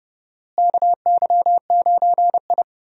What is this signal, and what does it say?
Morse code, heard as a keyed tone, about 29 wpm, 700 Hz. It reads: KY9I
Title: USA Call Signs - 4 Characters - Repeated 3x - 30wpm